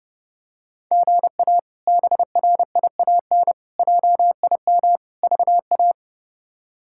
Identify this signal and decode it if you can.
Morse code, heard as a keyed tone, about 30 wpm, 700 Hz. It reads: GA BRIAN JIM VA
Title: K1USN SST QSO Practice - 30wpm